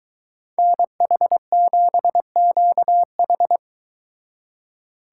Morse code, heard as a keyed tone, about 23 wpm, 700 Hz. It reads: NH7QH